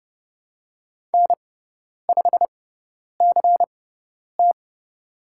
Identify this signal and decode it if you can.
Morse code, heard as a keyed tone, about 30 wpm, spaced as for 10 wpm, 700 Hz. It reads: N5CT
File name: Lesson 16g - Letter c - Call Signs 30wpm